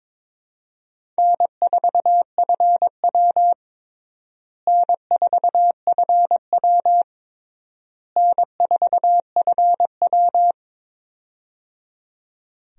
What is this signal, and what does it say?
Morse code, heard as a keyed tone, about 22 wpm, 700 Hz. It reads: N4FW N4FW N4FW
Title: USA Call Signs - 4 Characters - Repeated 3x - 22wpm